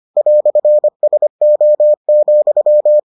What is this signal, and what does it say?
Morse code, heard as a keyed tone, about 25 wpm, 600 Hz. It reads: "SO,